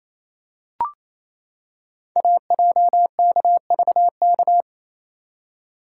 Morse code, heard as a keyed tone, about 27 wpm, 700 Hz. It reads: AJKVK
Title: Code Groups - 5 Letters - 28wpm